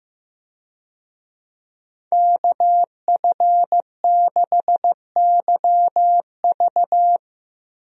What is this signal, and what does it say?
Morse code, heard as a keyed tone, about 16 wpm, 700 Hz. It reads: KF6YV
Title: USA Call Signs - 5 Characters - 15wpm